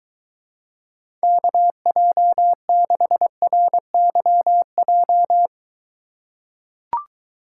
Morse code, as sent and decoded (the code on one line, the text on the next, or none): -.- .--- -.... .-. -.-- .---
KJ6RYJ